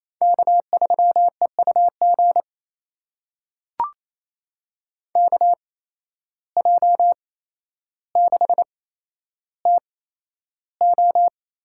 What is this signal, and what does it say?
Morse code, heard as a keyed tone, about 28 wpm, spaced as for 12 wpm, 700 Hz. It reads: K3EUG KJ6TO
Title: USA Call Signs - 5 Characters - 8x Character Spacing - 28wpm